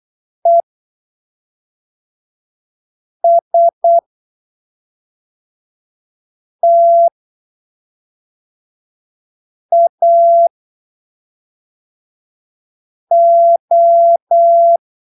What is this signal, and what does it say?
Morse code, heard as a keyed tone, about 8 wpm, spaced as for 3 wpm, 680 Hz. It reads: ESTAO
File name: arquivos / texto / texto 01 08ppm